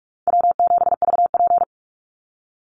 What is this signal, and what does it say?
Morse code, heard as a keyed tone, about 44 wpm, 700 Hz. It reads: W7VP
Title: Call Signs Repeated 3x - 45wpm